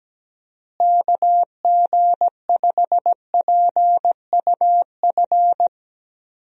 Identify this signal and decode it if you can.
Morse code, heard as a keyed tone, about 17 wpm, 700 Hz. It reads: KG5PUF
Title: USA Call Signs - 6 Characters - Repeated 2x - 17wpm